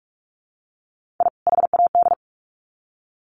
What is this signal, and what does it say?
Morse code, heard as a keyed tone, about 44 wpm, 700 Hz. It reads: I HAD
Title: Sets of 2 Words 45wpm